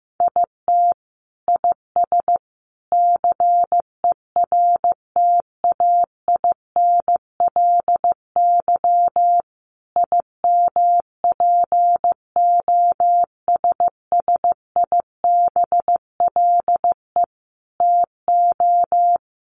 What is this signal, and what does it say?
Morse code, heard as a keyed tone, about 15 wpm, 700 Hz. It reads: IT IS CERTAINLY IMPOSSIBLE TO
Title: Sentences from Top 1100 Words 15wpm